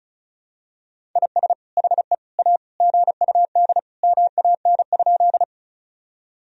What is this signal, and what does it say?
Morse code, heard as a keyed tone, about 36 wpm, 700 Hz. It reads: IS HE A GUD MAN?
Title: Sentences from Top 200 Words and Abbreviations 35wpm